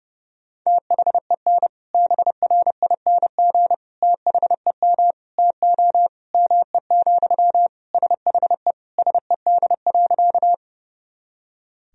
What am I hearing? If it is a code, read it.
Morse code, heard as a keyed tone, about 30 wpm, 700 Hz. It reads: THEN BRING THEM TO ME, SHE SED.
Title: Sentences from Top 500 Words and Abbreviations 30wpm